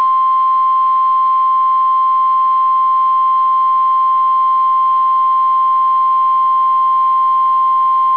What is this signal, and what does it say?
Emergency Alert System (SAME) tones (the 1050 Hz attention tone). It is heard from 0.5 to 7.0 s.